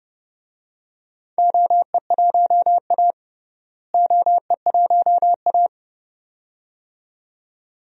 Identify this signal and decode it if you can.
Morse code, heard as a keyed tone, about 32 wpm, 700 Hz. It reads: OE1A OE1A
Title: DXCC-100 Call Signs - 4 Characters - Repeated 2x - 30wpm